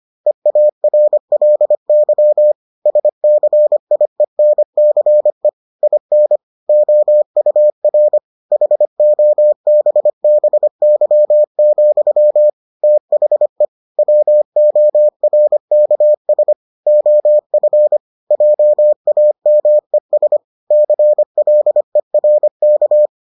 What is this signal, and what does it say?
Morse code, heard as a keyed tone, about 25 wpm, 600 Hz. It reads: EARLY SCIENCE IN OUR HOBBY, THE WORKS OF JAMES CLERK